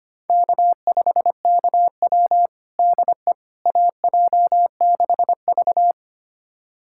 Morse code, heard as a keyed tone, about 25 wpm, 700 Hz. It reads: K5KW DE AJ6V
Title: USA Callsign DE Callsign - 4 Characters - 25wpm